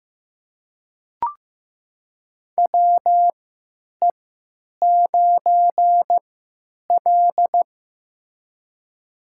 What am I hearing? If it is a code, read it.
Morse code, heard as a keyed tone, about 15 wpm, spaced as for 9 wpm, 700 Hz. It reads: WE9L
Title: USA Call Signs - 4 Characters - 3x Character Spacing - 15wpm